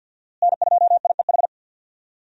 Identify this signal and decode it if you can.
Morse code, heard as a keyed tone, about 49 wpm, 700 Hz. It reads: N2IEH
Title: USA Call Signs - 5 Characters - Repeated 3x - 50wpm